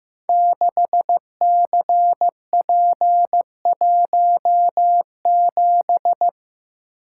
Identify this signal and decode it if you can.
Morse code, heard as a keyed tone, about 15 wpm, 700 Hz. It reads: 6CP17
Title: Code Groups - 5 Letter:Numbers - 15wpm